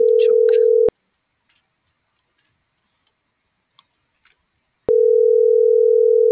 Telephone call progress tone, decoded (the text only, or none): ringback tone